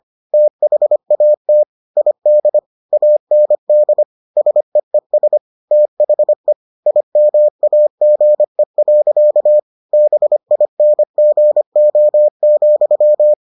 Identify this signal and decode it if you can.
Morse code, heard as a keyed tone, about 25 wpm, 600 Hz. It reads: THAT ID AND SEES THE IMAGE. BINGO,